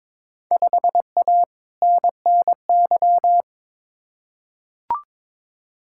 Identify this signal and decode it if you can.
Morse code, heard as a keyed tone, about 22 wpm, 700 Hz. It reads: 5A NNY
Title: Field Day Exchanges - 22wpm